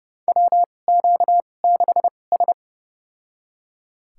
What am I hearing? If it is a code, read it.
Morse code, heard as a keyed tone, about 30 wpm, spaced as for 22 wpm, 700 Hz. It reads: WQ6S
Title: USA Call Signs - 2x Character Spacing - 30wpm